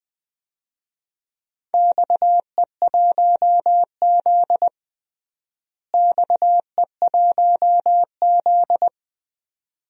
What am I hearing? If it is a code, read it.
Morse code, heard as a keyed tone, about 21 wpm, 700 Hz. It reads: XE1Z XE1Z
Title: DXCC-100 Call Signs - 4 Characters - Repeated 2x - 20wpm